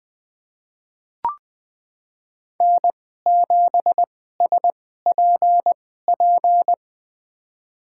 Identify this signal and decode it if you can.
Morse code, heard as a keyed tone, about 21 wpm, spaced as for 15 wpm, 700 Hz. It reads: N7SPP
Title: USA Call Signs - 5 Characters - 2x Character Spacing - 20wpm